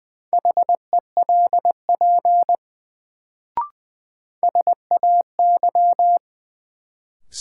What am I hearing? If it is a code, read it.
Morse code, heard as a keyed tone, about 20 wpm, 700 Hz. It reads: HELP SAY
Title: Sets of 2 Words - New Word Drill - 20wpm